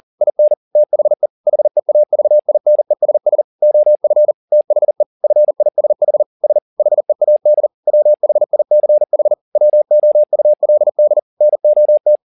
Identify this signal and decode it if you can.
Morse code, heard as a keyed tone, about 40 wpm, 600 Hz. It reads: IN THE HEAVINESS OF THE FISH S HEAD WHICH WOULD NOT
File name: the old man and the sea 2 40wpm